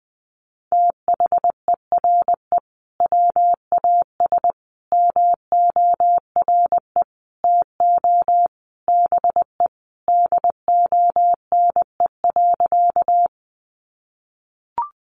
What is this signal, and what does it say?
Morse code, heard as a keyed tone, about 20 wpm, 700 Hz. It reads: THERE WAS MORE TO BE DONE.
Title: Sentences from Top 500 Words 20wpm